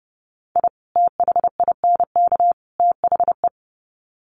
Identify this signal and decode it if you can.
Morse code, heard as a keyed tone, about 30 wpm, 700 Hz. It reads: I THINK THE
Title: Sets of 3 Words 30wpm